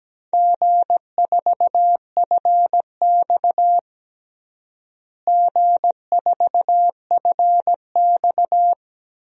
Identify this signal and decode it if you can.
Morse code, heard as a keyed tone, about 17 wpm, 700 Hz. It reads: G4FX G4FX
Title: DXCC-100 Call Signs - 4 Characters - Repeated 2x - 17wpm